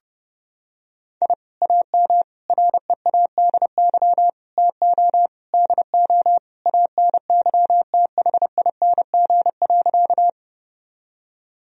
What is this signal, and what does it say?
Morse code, heard as a keyed tone, about 31 wpm, 700 Hz. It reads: I AM READY TO DO ANYTHING.
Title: Sentences from Top 1000 Words - Review Part 2 - 30wpm